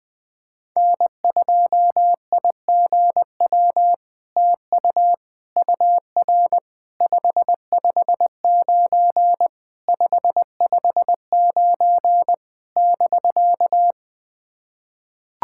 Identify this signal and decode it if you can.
Morse code, heard as a keyed tone, about 20 wpm, 700 Hz. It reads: N2IGW TU UR 559 559 <BK>
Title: POTA QSO Practice - 20wpm